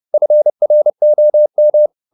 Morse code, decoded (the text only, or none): FROM